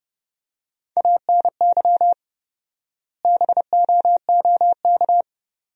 Morse code, heard as a keyed tone, about 32 wpm, 700 Hz. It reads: ANY BOOK
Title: 4x Word-Spacing - Sentences from Top 300 Words - 30wpm